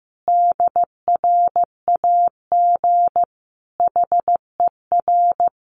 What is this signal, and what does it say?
Morse code, heard as a keyed tone, about 14 wpm, 700 Hz. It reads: DRAG HER
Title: Sentences from Top 1500 Words 15wpm